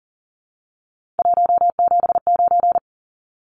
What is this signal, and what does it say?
Morse code, heard as a keyed tone, about 40 wpm, 700 Hz. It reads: 179
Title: Numbers - 3-Digit 40wpm